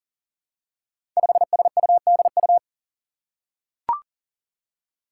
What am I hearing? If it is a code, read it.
Morse code, heard as a keyed tone, about 37 wpm, 700 Hz. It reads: 5SUDU
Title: Five Challenging Letter-Number Combinations 40wpm